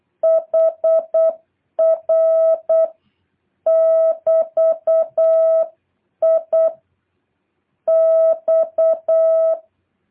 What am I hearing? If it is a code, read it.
Morse code, heard as a keyed tone, about 8 wpm, 650 Hz. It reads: HR <BT>I X
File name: jh8cbh / 2023cw